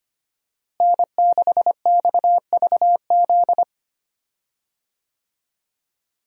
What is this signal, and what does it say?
Morse code, heard as a keyed tone, about 25 wpm, 700 Hz. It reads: N6XVZ
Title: USA Call Signs - 5 Characters - Repeated 3x - 25wpm